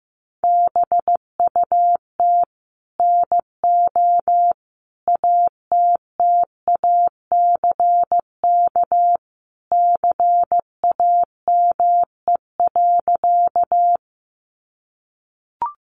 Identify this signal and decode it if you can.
Morse code, heard as a keyed tone, about 15 wpm, 700 Hz. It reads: BUT NO ATTACK CAME.
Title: Sentences from Top 1500 Words 15wpm